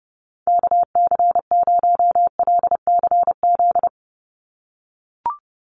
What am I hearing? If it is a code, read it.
Morse code, heard as a keyed tone, about 30 wpm, 700 Hz. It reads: KC0LCZ
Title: USA Call Signs 30wpm